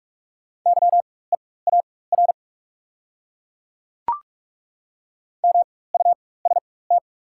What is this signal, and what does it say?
Morse code, heard as a keyed tone, about 43 wpm, spaced as for 21 wpm, 700 Hz. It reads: YEAR MUST
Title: 4x Character Spacing - Top 200 Words - 45wpm